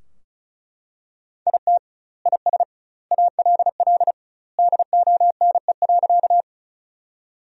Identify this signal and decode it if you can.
Morse code, heard as a keyed tone, about 36 wpm, 700 Hz. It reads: IT IS ALL DONE.
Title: 2x Word-Spacing - Sentences from Top 500 Words - 35wpm